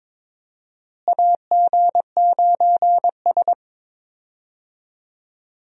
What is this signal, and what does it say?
Morse code, heard as a keyed tone, about 22 wpm, 700 Hz. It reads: AG9S
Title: USA Call Signs - 4 Characters - Repeated 2x - 22wpm